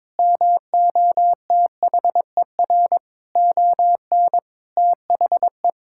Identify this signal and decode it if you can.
Morse code, heard as a keyed tone, about 22 wpm, 700 Hz. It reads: MOTHER ON THE
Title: Sentences from Top 1000 Words - Review Part 6 - 22wpm